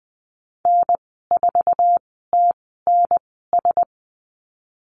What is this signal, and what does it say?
Morse code, heard as a keyed tone, about 20 wpm, spaced as for 14 wpm, 700 Hz. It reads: N4TNS
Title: 069 Lesson 12g - Letter d - Call Signs 20wpm